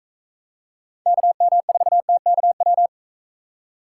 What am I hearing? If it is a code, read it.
Morse code, heard as a keyed tone, about 43 wpm, 700 Hz. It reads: KM4TKW